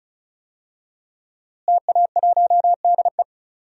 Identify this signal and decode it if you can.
Morse code, heard as a keyed tone, about 36 wpm, 700 Hz. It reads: TA1DE
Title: DXCC-100 Call Signs - 5 Characters - Repeated 2x - 35wpm